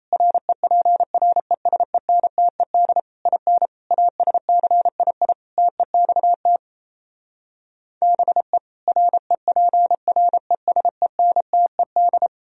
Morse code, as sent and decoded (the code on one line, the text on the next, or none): .-. .--. .-. ... -. - . -.. / .. -. / .- ... -.-. .. .. / - . -..- - / -... / .-. .--. .-. ... -. - . -..
REPRESENTED IN ASCII TEXT BE REPRESENTED